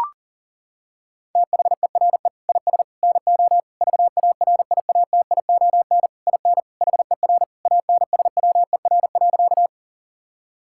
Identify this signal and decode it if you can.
Morse code, heard as a keyed tone, about 40 wpm, 700 Hz. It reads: THERE IS NO VARIATION IN HER ANSWER.
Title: Sentences from Top 1600 Words 40wpm